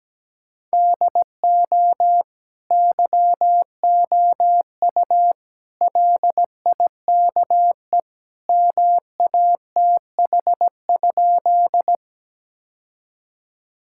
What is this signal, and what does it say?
Morse code, heard as a keyed tone, about 17 wpm, 700 Hz. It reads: DO YOU LIKE MATH?
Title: Sentences from Top 1000 Words - Review Part 6 - 17wpm